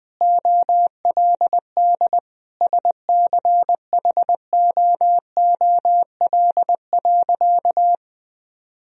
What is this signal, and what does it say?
Morse code, heard as a keyed tone, about 20 wpm, 700 Hz. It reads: OLD SCHOOL.
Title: Sentences from Top 1000 Words - Review Part 1 - 20wpm